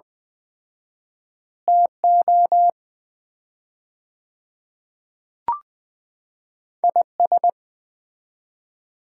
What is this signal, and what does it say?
Morse code, heard as a keyed tone, about 20 wpm, 700 Hz. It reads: TO IS
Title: 4x Word-Spacing - Sets of 2 Words - 20wpm